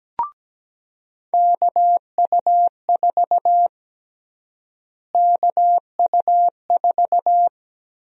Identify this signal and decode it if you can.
Morse code, heard as a keyed tone, about 17 wpm, 700 Hz. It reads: KU4 KU4
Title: USA Call Sign Prefixes - Repeated 2x - 17wpm